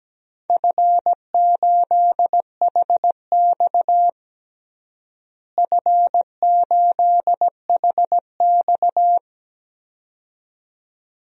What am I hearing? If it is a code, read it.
Morse code, heard as a keyed tone, about 17 wpm, 700 Hz. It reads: F8HX F8HX
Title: DXCC-100 Call Signs - 4 Characters - Repeated 2x - 17wpm